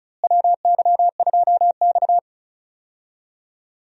Morse code, heard as a keyed tone, about 36 wpm, 700 Hz. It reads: WY2X
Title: USA Call Signs - 4 Characters - Repeated 2x - 35wpm